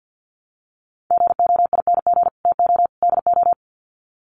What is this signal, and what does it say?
Morse code, heard as a keyed tone, about 49 wpm, 700 Hz. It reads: GOING TO DO
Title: Sets of 3 Words 50wpm